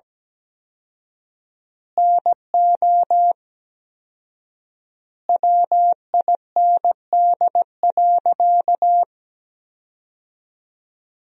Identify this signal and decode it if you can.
Morse code, heard as a keyed tone, about 17 wpm, 700 Hz. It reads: NO WIND.
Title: 4x Word-Spacing - Sentences from Top 400 Words - 17wpm